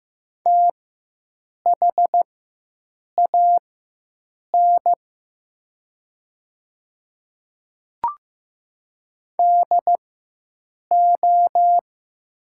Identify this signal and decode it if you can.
Morse code, heard as a keyed tone, about 15 wpm, spaced as for 7 wpm, 700 Hz. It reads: THAN DO
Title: 4x Character Spacing - Top 100 Words - 15wpm